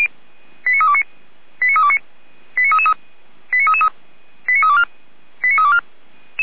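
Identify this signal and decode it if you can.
Five-tone selective call (ZVEI), heard as quick runs of five tones, tones of about 80 ms. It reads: 89329 89329 89333 89333 89324 89324
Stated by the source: ZVEI five-tone call received on SDR.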